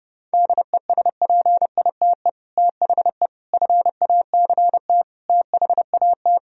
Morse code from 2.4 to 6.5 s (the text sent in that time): THE FACT THAT